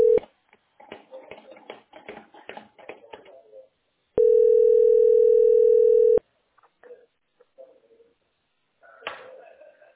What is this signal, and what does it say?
Ringback tone (North America), 440 + 480 Hz.
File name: 5291aeb3-6831-42dd-95e5